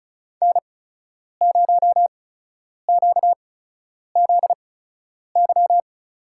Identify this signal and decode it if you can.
Morse code, heard as a keyed tone, about 34 wpm, spaced as for 10 wpm, 700 Hz. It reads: N0QZY
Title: USA Call Signs - 5 Characters - 8x Character Spacing - 35wpm